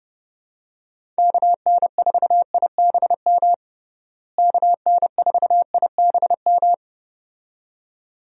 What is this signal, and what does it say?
Morse code, heard as a keyed tone, about 30 wpm, 700 Hz. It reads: KN4IBM KN4IBM
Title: USA Call Signs - 6 Characters - Repeated 2x - 30wpm